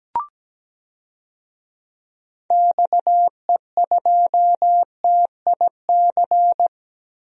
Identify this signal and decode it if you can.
Morse code, heard as a keyed tone, about 17 wpm, 700 Hz. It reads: XE2TIC
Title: DXCC-100 Call Signs - 6 Characters - 17wpm